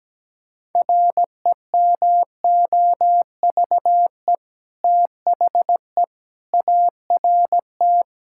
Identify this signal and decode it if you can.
Morse code, heard as a keyed tone, about 17 wpm, 700 Hz. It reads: REMOVE THE ART
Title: Sentences from Top 1000 Words - Review Part 6 - 17wpm